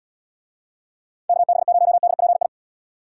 Morse code, heard as a keyed tone, about 78 wpm, 700 Hz. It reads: KB0DPI